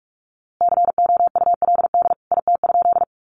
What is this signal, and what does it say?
Morse code, heard as a keyed tone, about 44 wpm, 700 Hz. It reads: COULD IT?
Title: A Sound Of Thunder 45wpm